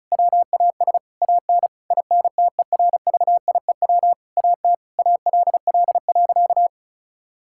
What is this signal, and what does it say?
Morse code, heard as a keyed tone, about 36 wpm, 700 Hz. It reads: WAS AN INTERVIEW AT ALL.
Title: Sentences from Top 1600 Words 35wpm